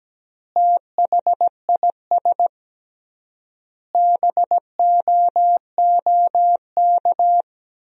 Morse code, heard as a keyed tone, about 17 wpm, 700 Hz. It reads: THIS BOOK
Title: 3x Word-Spacing - Sentences from Top 400 Words - 17wpm